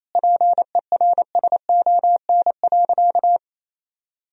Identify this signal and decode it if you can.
Morse code, heard as a keyed tone, about 28 wpm, 700 Hz. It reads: PERSON.
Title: Sentences from Top 1000 Words - Review Part 3 - 28wpm